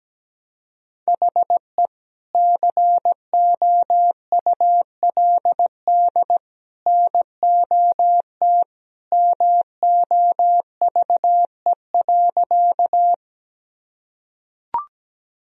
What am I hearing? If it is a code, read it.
Morse code, heard as a keyed tone, about 17 wpm, 700 Hz. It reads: HE COULD NOT MOVE.
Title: Sentences from Top 1000 Words - Review Part 1 - 17wpm